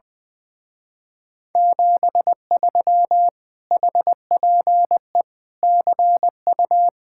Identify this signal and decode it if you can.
Morse code, heard as a keyed tone, about 21 wpm, 700 Hz. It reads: 73 HPE CU
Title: Speed-Racing Common QSO Phrases Encore - 20wpm